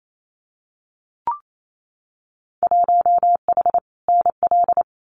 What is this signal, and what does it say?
Morse code, heard as a keyed tone, about 28 wpm, 700 Hz. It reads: 1H NL
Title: Winter Field Day Exchanges 28wpm